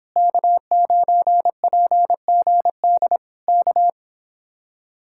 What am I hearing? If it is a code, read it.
Morse code, heard as a keyed tone, about 26 wpm, 700 Hz. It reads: K9PGD K